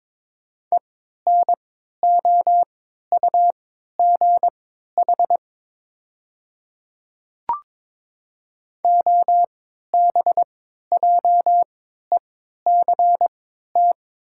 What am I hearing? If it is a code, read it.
Morse code, heard as a keyed tone, about 22 wpm, spaced as for 12 wpm, 700 Hz. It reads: ENOUGH OBJECT
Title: 3x Character Spacing - Top 500 Words - 22wpm